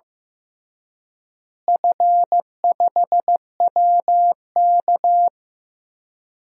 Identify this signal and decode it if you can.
Morse code, heard as a keyed tone, about 15 wpm, 700 Hz. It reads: F5WK
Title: DXCC-100 Call Signs - 4 Characters - Repeated 2x - 15wpm